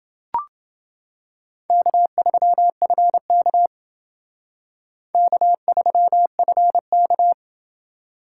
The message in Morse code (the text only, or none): K3FK K3FK